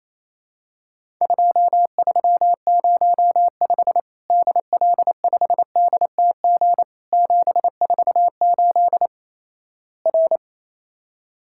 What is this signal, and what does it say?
Morse code, heard as a keyed tone, about 28 wpm, 700 Hz. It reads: 2305 DL5DTG 748 R
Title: Worked All Europe DX Contest - QTC Exchange - 28wpm